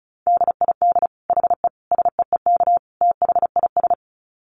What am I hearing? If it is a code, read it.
Morse code, heard as a keyed tone, about 34 wpm, 700 Hz. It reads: DID HE SEEK THIS
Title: Dit Challenge 35wpm